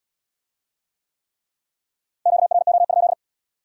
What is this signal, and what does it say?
Morse code, heard as a keyed tone, about 77 wpm, 700 Hz. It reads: QRO?